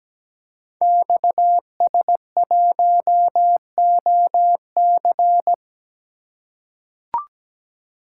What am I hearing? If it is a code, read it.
Morse code, heard as a keyed tone, about 17 wpm, 700 Hz. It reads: XS1OC
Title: Code Groups - 5 Letter:Numbers - 17wpm